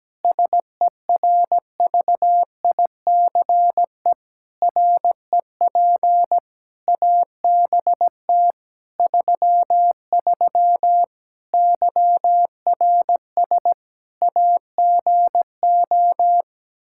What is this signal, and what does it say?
Morse code, heard as a keyed tone, about 17 wpm, 700 Hz. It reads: SERVICE REP ABT 33 YRS AGO
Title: Practice QSO - Exchange 3 - More Info - 17wpm